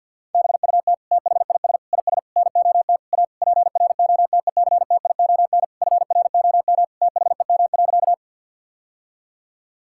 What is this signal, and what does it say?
Morse code, heard as a keyed tone, about 50 wpm, 700 Hz. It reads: BUT THIS IS NOT A PROTECTION FROM THEM.
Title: Sentences from Top 1600 Words 50wpm